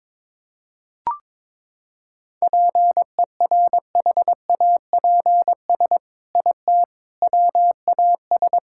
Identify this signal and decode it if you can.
Morse code, heard as a keyed tone, about 22 wpm, 700 Hz. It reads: PERHAPS IT WAS
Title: Sentences from Top 1600 Words 22wpm